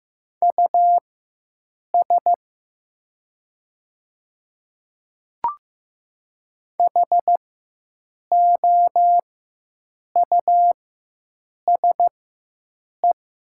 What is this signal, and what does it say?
Morse code, heard as a keyed tone, about 15 wpm, spaced as for 7 wpm, 700 Hz. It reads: US HOUSE
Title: 4x Character Spacing - Top 200 Words - 15wpm